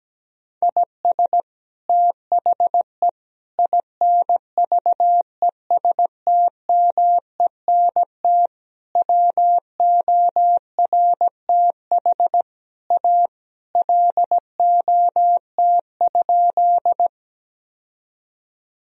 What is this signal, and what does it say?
Morse code, heard as a keyed tone, about 17 wpm, 700 Hz. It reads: IS THE INVESTMENT WORTH A LOT?
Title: Sentences from Top 1000 Words - Review Part 4 - 17wpm